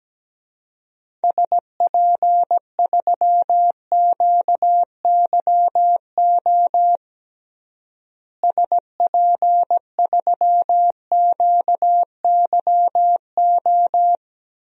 Morse code, heard as a keyed tone, about 17 wpm, 700 Hz. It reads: SP3QYO SP3QYO
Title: DXCC-100 Call Signs - 6 Characters - Repeated 3x - 17wpm